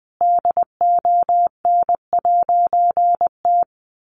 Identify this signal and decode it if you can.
Morse code, heard as a keyed tone, about 20 wpm, 700 Hz. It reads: DON'T
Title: A Sound Of Thunder 20wpm